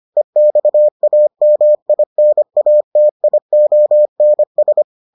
Morse code, heard as a keyed tone, about 25 wpm, 600 Hz. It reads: EXAMINATIONS